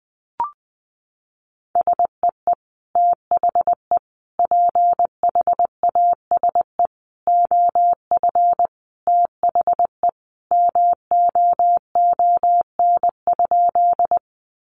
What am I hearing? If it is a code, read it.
Morse code, heard as a keyed tone, about 20 wpm, 700 Hz. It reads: SEE THE PHASE OF THE MOON?
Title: Sentences from Top 1200 Words 20wpm